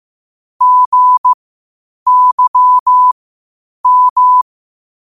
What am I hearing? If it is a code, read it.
Morse code, heard as a keyed tone, about 15 wpm, spaced as for 8 wpm, 1000 Hz. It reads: GYM
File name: qotdfiles / 8wpm